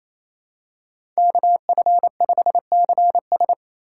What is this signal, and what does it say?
Morse code, heard as a keyed tone, about 28 wpm, 700 Hz. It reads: KF5CS
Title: USA Call Signs - 5 Characters - 2x Character Spacing - 28wpm